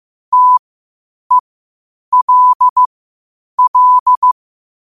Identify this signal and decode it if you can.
Morse code, heard as a keyed tone, about 15 wpm, spaced as for 8 wpm, 1000 Hz. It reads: TELL
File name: qotdfiles / 8wpm